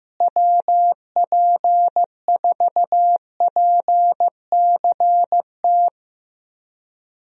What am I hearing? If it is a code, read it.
Morse code, heard as a keyed tone, about 16 wpm, 700 Hz. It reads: WP4PCT
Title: USA Call Signs - 6 Characters - Repeated 3x - 15wpm